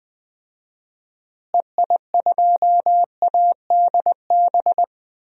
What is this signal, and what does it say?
Morse code, heard as a keyed tone, about 20 wpm, 700 Hz. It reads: EI2ADB
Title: DXCC-100 Call Signs - 6 Characters - Repeated 3x - 20wpm